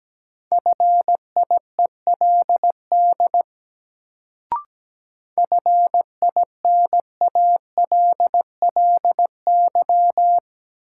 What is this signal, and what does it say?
Morse code, heard as a keyed tone, about 17 wpm, 700 Hz. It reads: FIELD FINALLY
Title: Sentences from Top 1000 Words - New Word Drill - 17wpm